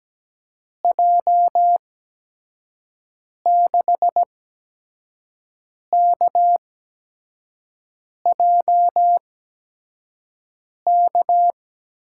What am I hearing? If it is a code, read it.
Morse code, heard as a keyed tone, about 17 wpm, spaced as for 5 wpm, 700 Hz. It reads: J6KJK
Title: USA Call Signs - 6 Characters - 8x Character Spacing - 17wpm